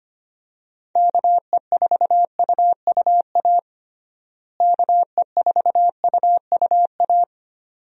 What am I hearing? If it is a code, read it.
Morse code, heard as a keyed tone, about 25 wpm, 700 Hz. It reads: KE4UUA KE4UUA
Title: USA Call Signs - 6 Characters - Repeated 2x - 25wpm